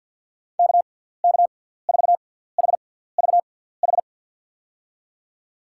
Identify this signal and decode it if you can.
Morse code, heard as a keyed tone, about 52 wpm, spaced as for 17 wpm, 700 Hz. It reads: KK4HVH
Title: USA Call Signs - 6 Characters - 6x Character Spacing - 50wpm